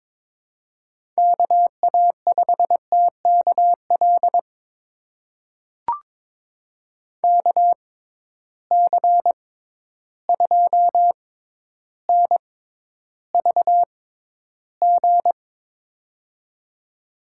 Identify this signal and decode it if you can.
Morse code, heard as a keyed tone, about 22 wpm, spaced as for 11 wpm, 700 Hz. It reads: KA5TKL KC2NVG
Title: USA Call Signs - 6 Characters - 6x Character Spacing - 22wpm